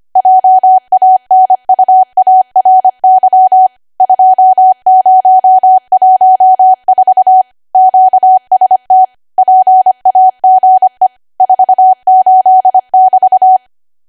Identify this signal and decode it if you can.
Morse code, heard as a keyed tone, about 25 wpm, 750 Hz. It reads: JANUARY 2014 QST PAGE 48<BT>
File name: Archive / 25 WPM / 25WPM